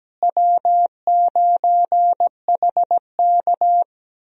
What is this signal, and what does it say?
Morse code, heard as a keyed tone, about 17 wpm, 700 Hz. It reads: W9HK
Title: USA Call Signs - 4 Characters - Repeated 2x - 17wpm